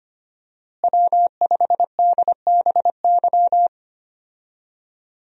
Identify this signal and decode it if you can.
Morse code, heard as a keyed tone, about 25 wpm, 700 Hz. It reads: W5DBY
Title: USA Call Signs - 5 Characters - Repeated 2x - 25wpm